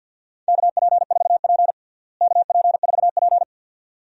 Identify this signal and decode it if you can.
Morse code, heard as a keyed tone, about 47 wpm, 700 Hz. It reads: KP4P KP4P
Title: DXCC-100 Call Signs - 4 Characters - Repeated 2x - 50wpm